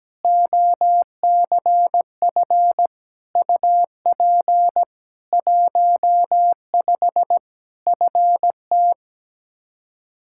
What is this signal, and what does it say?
Morse code, heard as a keyed tone, about 17 wpm, 700 Hz. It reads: OCF UP 15 FT
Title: K1USN SST QSO Practice 17wpm